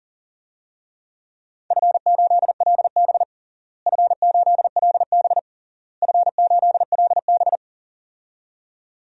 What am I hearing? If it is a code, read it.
Morse code, heard as a keyed tone, about 40 wpm, 700 Hz. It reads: F8LB F8LB F8LB
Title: DXCC-100 Call Signs - 4 Characters - Repeated 3x - 40wpm